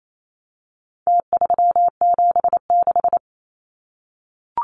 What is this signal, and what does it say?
Morse code, heard as a keyed tone, about 28 wpm, 700 Hz. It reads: T376
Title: Numbers - Contest Style - 4-Digit 28wpm